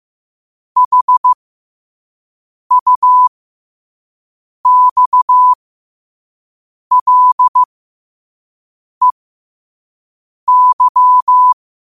Morse code, heard as a keyed tone, about 15 wpm, spaced as for 5 wpm, 1000 Hz. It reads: HUXLEY
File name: qotdfiles / 5wpm